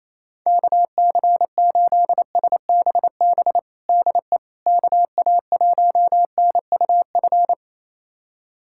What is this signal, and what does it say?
Morse code, heard as a keyed tone, about 28 wpm, 700 Hz. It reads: KC8SBB DE KA1NUF
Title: USA Callsign DE Callsign - 6 Characters - 28wpm